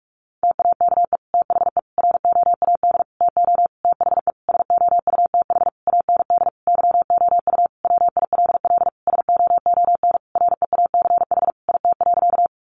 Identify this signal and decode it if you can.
Morse code, heard as a keyed tone, about 44 wpm, 700 Hz. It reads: TAKE THE ROAD TO THE SOUTH AND YOU WILL SOON REACH IT.
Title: Sentences from Top 400 Words 45wpm